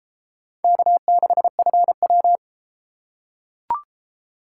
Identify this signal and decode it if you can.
Morse code, heard as a keyed tone, about 31 wpm, 700 Hz. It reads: K6FW